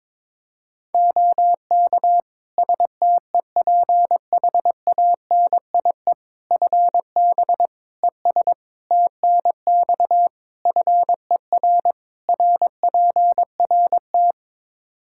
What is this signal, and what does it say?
Morse code, heard as a keyed tone, about 22 wpm, 700 Hz. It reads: OK STEPHANIE FB ES TNX FER RPRT
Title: Practice QSO - Exchange 2 - Info - 22wpm